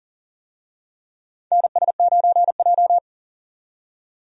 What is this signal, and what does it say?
Morse code, heard as a keyed tone, about 43 wpm, 690 Hz. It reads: NS9J
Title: USA Call Signs - 4 Characters - 40wpm